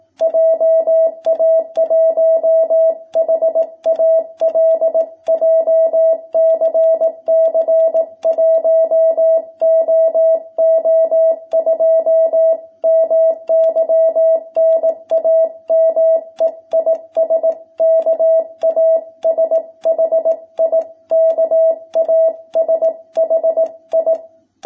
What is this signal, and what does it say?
Morse code, heard as a keyed tone, about 18 wpm, 660 Hz. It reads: JA1HAL JCC 1OO2 MY NAME IS KASHI KASHI
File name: jh8cbh / 2021cw